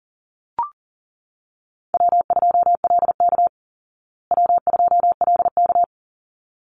Morse code, heard as a keyed tone, about 40 wpm, 700 Hz. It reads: W2LK W2LK
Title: Call Signs Repeated 3x - 40wpm